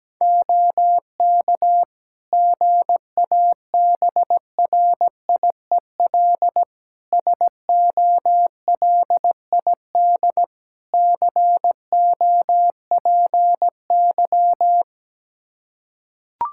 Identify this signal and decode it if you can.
Morse code, heard as a keyed tone, about 17 wpm, 700 Hz. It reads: OK GABRIEL SOLID COPY
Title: Practice QSO - Exchange 3 - More Info - 17wpm